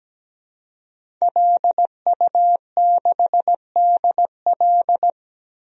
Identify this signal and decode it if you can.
Morse code, heard as a keyed tone, about 17 wpm, 700 Hz. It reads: LU6DL
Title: DXCC-100 Call Signs - 5 Characters - 17wpm